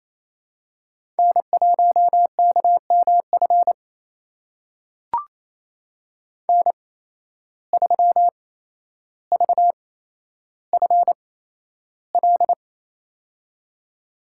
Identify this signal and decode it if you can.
Morse code, heard as a keyed tone, about 29 wpm, spaced as for 12 wpm, 700 Hz. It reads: N1KMF N3VFL
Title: USA Call Signs - 5 Characters - 8x Character Spacing - 28wpm